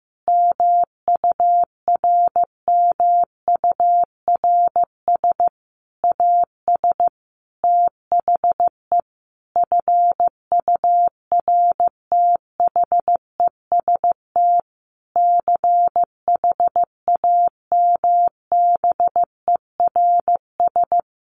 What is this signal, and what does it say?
Morse code, heard as a keyed tone, about 15 wpm, 700 Hz. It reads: MURMURS AS THE FURTHEST CHAMBERS
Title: A Sound Of Thunder 15wpm